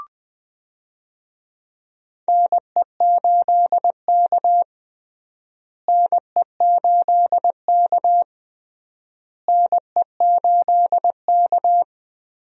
Morse code, heard as a keyed tone, about 20 wpm, 700 Hz. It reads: NE8K NE8K NE8K
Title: USA Call Signs - 4 Characters - Repeated 3x - 20wpm